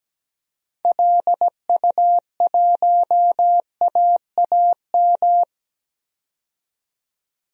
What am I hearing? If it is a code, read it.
Morse code, heard as a keyed tone, about 17 wpm, 700 Hz. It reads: LU1AAM
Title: DXCC-100 Call Signs - 6 Characters - 17wpm